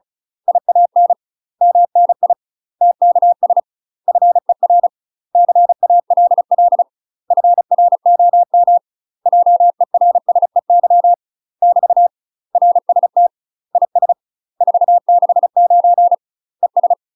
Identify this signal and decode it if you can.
Morse code, heard as a keyed tone, about 34 wpm, 700 Hz. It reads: IAN MNI TKS FER CALL FROM JERSEY <BT> RST IS 469 ES
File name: UK MASTER QSOs TRAINING / 35WPM / UK MASTER QSOs 0040